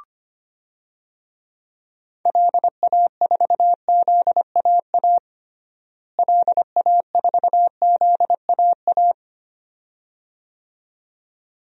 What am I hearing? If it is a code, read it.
Morse code, heard as a keyed tone, about 25 wpm, 700 Hz. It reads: LA4ZAA LA4ZAA
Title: DXCC-100 Call Signs - 6 Characters - Repeated 2x - 25wpm